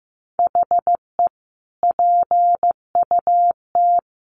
Morse code, heard as a keyed tone, about 14 wpm, 700 Hz. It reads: HE PUT
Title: Sentences from Top 1400 Words 15wpm